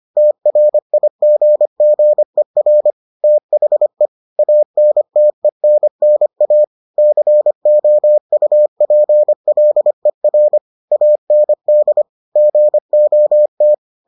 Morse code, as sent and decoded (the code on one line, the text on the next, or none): - .-. .. --. --. .-. / - .... / .- -. - . -. -. .- / -.-. --- ..- .--. .-.. .-. / .- -. -.. / --. --- -
TRIGGER THE ANTENNA COUPLER AND GOT